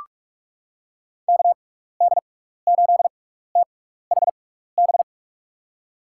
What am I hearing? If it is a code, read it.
Morse code, heard as a keyed tone, about 47 wpm, spaced as for 16 wpm, 700 Hz. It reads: KD8THB
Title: USA Call Signs - 6x Character Spacing - 45wpm